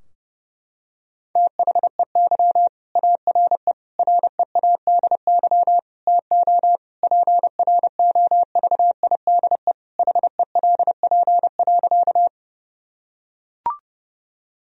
Morse code, heard as a keyed tone, about 30 wpm, 700 Hz. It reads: THEY ARE READY TO PROVIDE HELP.
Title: Sentences from Top 1000 Words - Review Part 3 - 30wpm